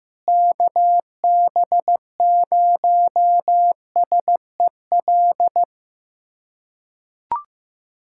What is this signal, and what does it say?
Morse code, heard as a keyed tone, about 15 wpm, 700 Hz. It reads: KB0SEL
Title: USA Callsign DE Callsign - 6 Characters - 15wpm